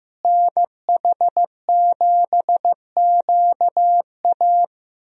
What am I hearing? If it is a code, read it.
Morse code, heard as a keyed tone, about 15 wpm, 700 Hz. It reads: NH7QA
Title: USA Call Signs - 5 Characters - Repeated 2x - 15wpm